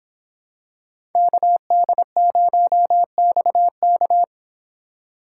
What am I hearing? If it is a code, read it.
Morse code, heard as a keyed tone, about 26 wpm, 700 Hz. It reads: KD0XK